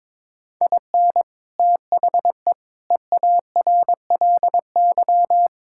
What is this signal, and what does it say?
Morse code, heard as a keyed tone, about 22 wpm, 700 Hz. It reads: IN THE EARLY
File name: practice / 22wpm / Sets of 6 Words - 22wpm